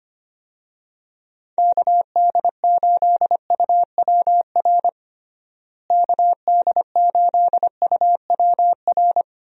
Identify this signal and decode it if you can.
Morse code, heard as a keyed tone, about 25 wpm, 700 Hz. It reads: KD8UWR KD8UWR
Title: USA Call Signs - 6 Characters - Repeated 2x - 25wpm